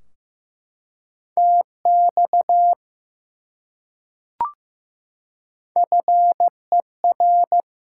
Morse code, heard as a keyed tone, about 15 wpm, 700 Hz. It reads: TX FER
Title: Top 90 QSO Elements 15wpm